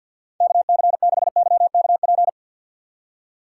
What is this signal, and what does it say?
Morse code, heard as a keyed tone, about 46 wpm, 700 Hz. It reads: KC6YKP
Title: USA Call Signs - 6 Characters - 6x Character Spacing - 50wpm